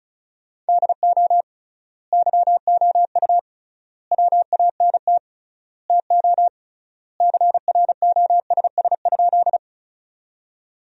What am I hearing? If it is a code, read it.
Morse code, heard as a keyed tone, about 36 wpm, 700 Hz. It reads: DO YOU WANT TO CROSS?
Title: 3x Word-Spacing - Sentences from Top 300 Words - 35wpm